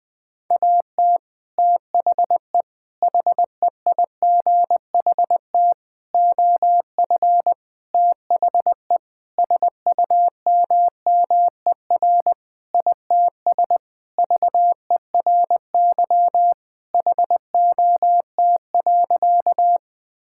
Morse code, heard as a keyed tone, about 20 wpm, 700 Hz. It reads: AT THE HEIGHT OF THE SUMMER ITS VERY HOT.
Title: Sentences with 5-Word Phrases - 20wpm